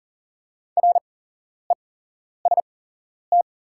Morse code, heard as a keyed tone, about 37 wpm, spaced as for 11 wpm, 700 Hz. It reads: REST
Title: 8x Character Spacing - Top 500 Words - 40wpm